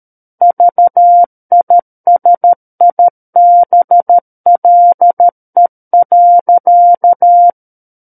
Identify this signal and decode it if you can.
Morse code, heard as a keyed tone, about 13 wpm, 700 Hz. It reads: VISIBLE.
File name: Book One / 04-Chapter 4 - 13 WPM